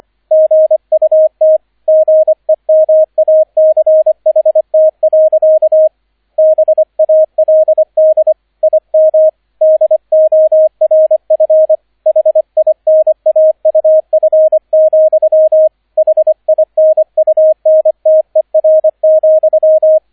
Morse code, heard as a keyed tone, about 25 wpm, 630 Hz, with radio noise. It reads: GUT GEMACHT. BALD IM DORF HINAUF, HINUNTER,
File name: 003 Dritter Streich 25wpm